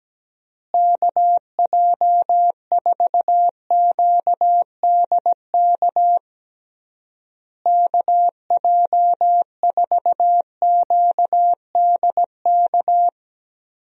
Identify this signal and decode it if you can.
Morse code, heard as a keyed tone, about 17 wpm, 700 Hz. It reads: KJ4QDK KJ4QDK
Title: USA Call Signs - 6 Characters - Repeated 3x - 17wpm